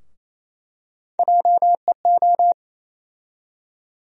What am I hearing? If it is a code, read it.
Morse code, heard as a keyed tone, about 28 wpm, 700 Hz. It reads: JEO
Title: USA Call Sign Postfixes - 3 Letters 28wpm